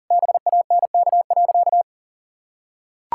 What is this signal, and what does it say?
Morse code, heard as a keyed tone, about 40 wpm, 700 Hz. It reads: BANK.
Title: Sentences from Top 1000 Words - Review Part 6 - 40wpm